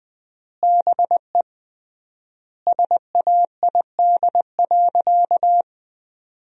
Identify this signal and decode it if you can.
Morse code, heard as a keyed tone, about 20 wpm, 700 Hz. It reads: BE SAID.
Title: 3x Word-Spacing - Sentences from Top 100 Words - 20wpm